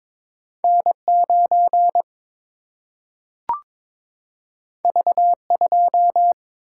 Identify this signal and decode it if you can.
Morse code, heard as a keyed tone, about 22 wpm, 700 Hz. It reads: N 9 V 2
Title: Single-Letter Single-Number 22wpm